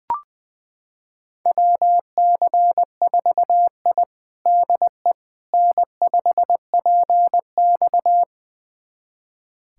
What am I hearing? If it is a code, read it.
Morse code, heard as a keyed tone, about 20 wpm, 700 Hz. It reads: WC4I DE N5PX
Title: USA Callsign DE Callsign - 4 Characters - 20wpm